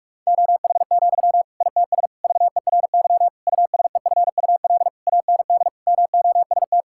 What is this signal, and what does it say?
Morse code, heard as a keyed tone, about 46 wpm, 700 Hz. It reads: OH, ITS VERY USEFUL AND MOST
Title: Sentences from Top 1000 Words - Review Part 6 - 45wpm